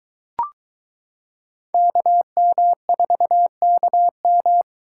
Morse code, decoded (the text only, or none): KM4KM